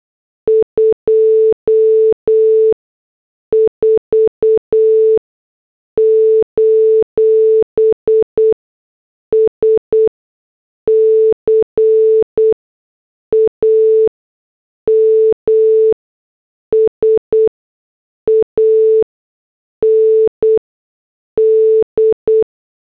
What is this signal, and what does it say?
Morse code, heard as a keyed tone, about 8 wpm, spaced as for 6 wpm, 440 Hz. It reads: 24: SCAMS AND